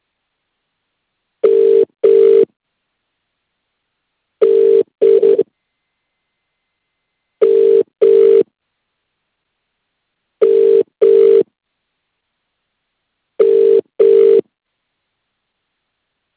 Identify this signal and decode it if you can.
Ringback tone (UK), 400 + 450 Hz.